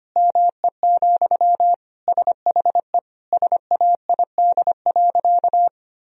Morse code, heard as a keyed tone, about 25 wpm, 700 Hz. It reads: ME, SHE SAID.
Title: Sentences from Top 1000 Words - Review Part 1 - 25wpm